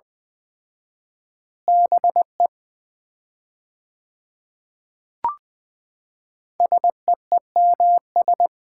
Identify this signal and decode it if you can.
Morse code, heard as a keyed tone, about 21 wpm, 700 Hz. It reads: BE SEEMS
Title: 4x Word-Spacing - Sets of 3 Words - 20wpm